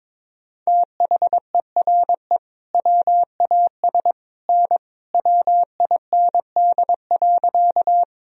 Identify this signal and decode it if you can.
Morse code, heard as a keyed tone, about 22 wpm, 700 Hz. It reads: THERE WAS N WIND.
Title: Sentences from Top 400 Words and Abbreviations 22wpm